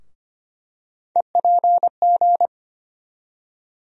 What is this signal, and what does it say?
Morse code, heard as a keyed tone, about 25 wpm, 700 Hz. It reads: EPG
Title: USA Call Sign Postfixes - 3 Letters 25wpm